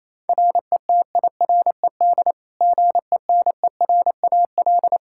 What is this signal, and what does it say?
Morse code, heard as a keyed tone, about 28 wpm, 700 Hz. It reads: RETIRED GENERAL
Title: Practice QSO - Exchange 3 - More Info - 28wpm